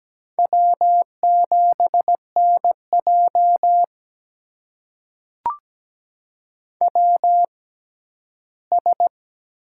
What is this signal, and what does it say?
Morse code, heard as a keyed tone, about 17 wpm, spaced as for 11 wpm, 700 Hz. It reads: W7NJ WS
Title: USA Call Signs - 4 Characters - 6x Character Spacing - 17wpm